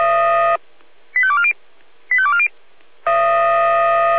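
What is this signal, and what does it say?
Five-tone selective call (ZVEI), heard as quick runs of five tones, tones of about 70 ms. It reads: 85240 85240
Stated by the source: ZVEI five-tone selective call received on SDR.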